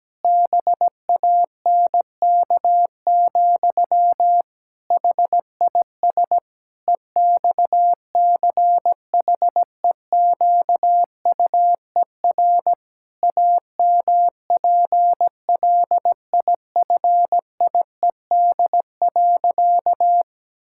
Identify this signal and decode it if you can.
Morse code, heard as a keyed tone, about 17 wpm, 700 Hz. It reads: BANK, HIS EXCHEQUER AMPLIFIED.
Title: Pangram Challenge v2 - 17wpm